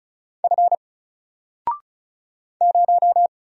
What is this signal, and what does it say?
Morse code, heard as a keyed tone, about 35 wpm, 700 Hz. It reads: F 0
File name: Single-Letter-Number-35wpm